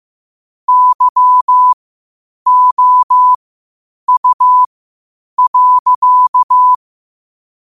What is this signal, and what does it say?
Morse code, heard as a keyed tone, about 15 wpm, spaced as for 8 wpm, 1000 Hz. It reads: YOU.
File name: qotdfiles / 8wpm